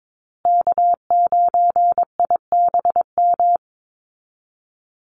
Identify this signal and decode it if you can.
Morse code, heard as a keyed tone, about 22 wpm, 700 Hz. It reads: K9IBM
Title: Call Signs Repeated 2x - 22wpm